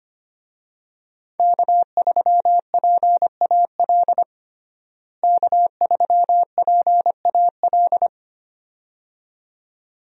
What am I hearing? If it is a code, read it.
Morse code, heard as a keyed tone, about 25 wpm, 700 Hz. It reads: K3PAL K3PAL
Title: USA Call Signs - 5 Characters - Repeated 2x - 25wpm